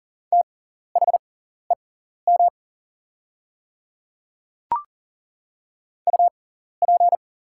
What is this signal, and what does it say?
Morse code, heard as a keyed tone, about 40 wpm, spaced as for 14 wpm, 700 Hz. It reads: THEM UP
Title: 6x Character Spacing - Top 100 Words - 40wpm